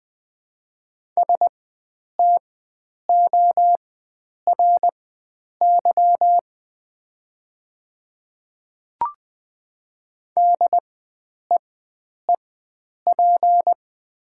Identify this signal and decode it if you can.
Morse code, heard as a keyed tone, about 20 wpm, spaced as for 9 wpm, 700 Hz. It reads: STORY DEEP
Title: 4x Character Spacing - Top 500 Words - 20wpm